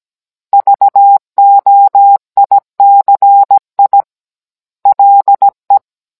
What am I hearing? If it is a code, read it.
Morse code, heard as a keyed tone, about 17 wpm, 800 Hz. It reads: VOICI LE
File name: Cours-NR18-16mm